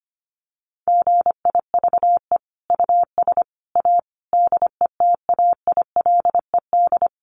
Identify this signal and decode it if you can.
Morse code, heard as a keyed tone, about 25 wpm, 700 Hz. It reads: GIVE US A DETAILED
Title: Sentences from Top 1500 Words 25wpm